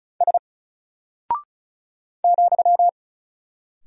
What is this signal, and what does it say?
Morse code, heard as a keyed tone, about 36 wpm, 700 Hz. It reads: S ,
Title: Single Character (Letter:Number:Punctuation:Prosign) Alt 35wpm